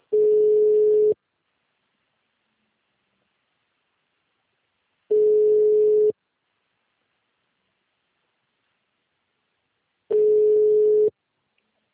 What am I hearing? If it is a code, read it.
Ringback tone (Europe, 425 Hz).